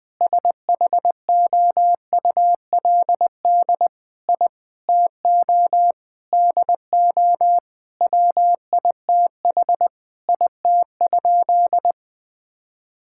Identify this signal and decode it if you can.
Morse code, heard as a keyed tone, about 20 wpm, 700 Hz. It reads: SHOULD I TO DO WITH IT?
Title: Sentences with 4-Word Phrases - 20wpm